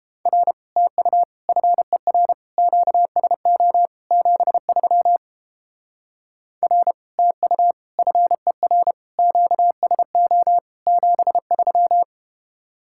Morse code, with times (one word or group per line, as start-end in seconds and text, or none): 0.3-0.5 R
0.8-1.2 TU
1.5-2.3 FER
2.6-3.9 QSO
4.1-5.2 73
6.6-6.9 R
7.2-7.7 TU
8.0-8.9 FER
9.2-10.6 QSO
10.9-12.0 73